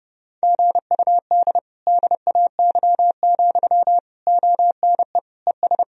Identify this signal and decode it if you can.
Morse code, heard as a keyed tone, about 32 wpm, 700 Hz. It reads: GUD DAY, ONE ES
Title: Sentences from Top 200 Words and Abbreviations 30wpm